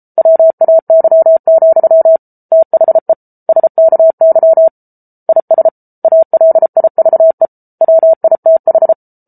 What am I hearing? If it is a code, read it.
Morse code, heard as a keyed tone, about 34 wpm, 650 Hz. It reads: WAY, THE SKY IS ALIVE WITH